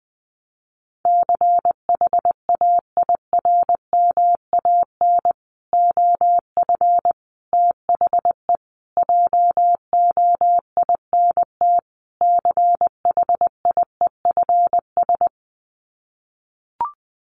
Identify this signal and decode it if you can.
Morse code, heard as a keyed tone, about 20 wpm, 700 Hz. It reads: CHAIRMAN OF THE JOINT CHIEFS
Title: Sets of 5 Words Encore 20wpm